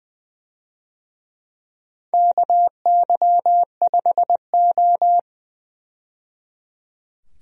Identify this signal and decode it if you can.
Morse code, heard as a keyed tone, about 21 wpm, 700 Hz. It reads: KY5O
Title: USA Call Signs - 4 Characters - 20wpm